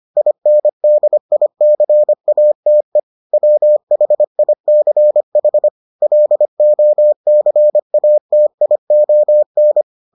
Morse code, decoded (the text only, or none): INDICATE WHICH LOCATION